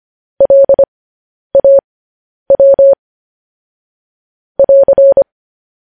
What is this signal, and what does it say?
Morse code, heard as a keyed tone, about 25 wpm, spaced as for 10 wpm, 550 Hz. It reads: LAW <AR>
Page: news headlines in morse code at 10 wpm